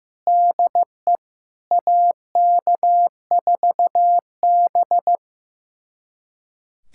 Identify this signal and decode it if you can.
Morse code, heard as a keyed tone, about 15 wpm, 700 Hz. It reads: DE AK4B
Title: USA Callsign DE Callsign - 4 Characters - 15wpm